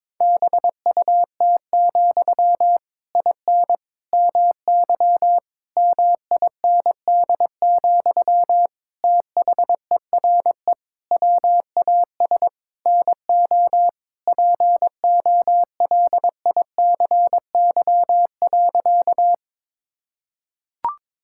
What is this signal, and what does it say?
Morse code, heard as a keyed tone, about 22 wpm, 700 Hz. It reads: BUT, IN MY MIND, THERE WAS NO POLICY.
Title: Sentences from Top 1000 Words - Review Part 5 - 22wpm